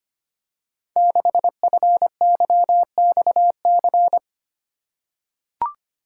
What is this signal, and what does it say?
Morse code, heard as a keyed tone, about 25 wpm, 700 Hz. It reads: 6FYXC
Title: Code Groups - 5 Letter:Numbers - 25wpm